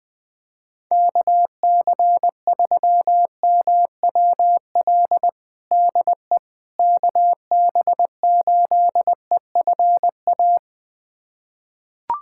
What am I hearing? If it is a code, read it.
Morse code, heard as a keyed tone, about 21 wpm, 700 Hz. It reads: KC3MWL DE KB8EFA
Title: USA Callsign DE Callsign - 6 Characters - 20wpm